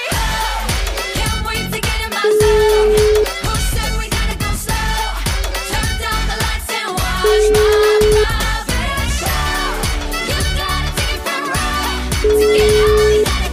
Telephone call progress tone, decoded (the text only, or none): ringback tone